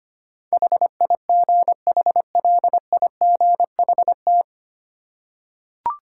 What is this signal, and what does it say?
Morse code, heard as a keyed tone, about 25 wpm, 700 Hz. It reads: HIGHLIGHT
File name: Nine Letter Words - 25wpm